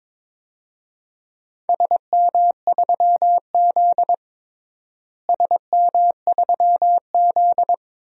Morse code, heard as a keyed tone, about 22 wpm, 700 Hz. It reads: SM3Z SM3Z
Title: DXCC-100 Call Signs - 4 Characters - Repeated 2x - 22wpm